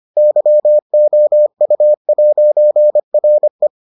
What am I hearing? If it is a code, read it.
Morse code, heard as a keyed tone, about 25 wpm, 600 Hz. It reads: YOU'RE